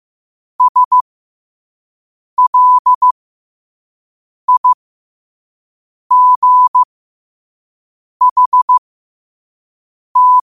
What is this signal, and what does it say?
Morse code, heard as a keyed tone, about 15 wpm, spaced as for 5 wpm, 1000 Hz. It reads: SLIGHT
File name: qotdfiles / 5wpm